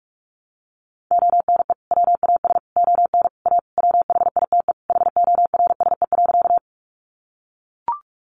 Morse code, heard as a keyed tone, about 44 wpm, 700 Hz. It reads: ONE WAS ON A WHITE HORSE.
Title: Sentences from Top 300 Words 45wpm